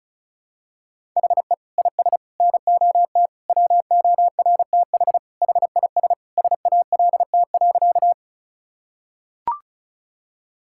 Morse code, heard as a keyed tone, about 36 wpm, 700 Hz. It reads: HE IS NOT WORTH HIS SALT.
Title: Sentences from Top 1000 Words - Review Part 5 - 35wpm